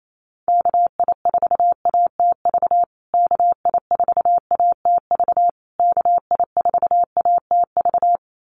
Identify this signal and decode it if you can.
Morse code, heard as a keyed tone, about 28 wpm, 700 Hz. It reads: KI4ATV KI4ATV KI4ATV
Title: USA Call Signs Repeated 3x 28wpm